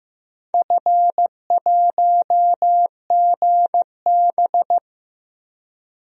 Morse code, heard as a keyed tone, about 15 wpm, 700 Hz. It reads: F1GB
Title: DXCC-100 Call Signs - 4 Characters - 15wpm